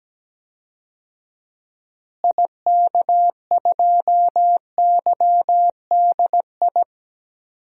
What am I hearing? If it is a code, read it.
Morse code, heard as a keyed tone, about 17 wpm, 700 Hz. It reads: IK2YDI
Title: DXCC-100 Call Signs - 6 Characters - Repeated 3x - 17wpm